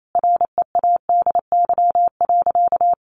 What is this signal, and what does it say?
Morse code, heard as a keyed tone, about 28 wpm, 700 Hz. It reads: READY.
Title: Sentences from Top 700 Words 28wpm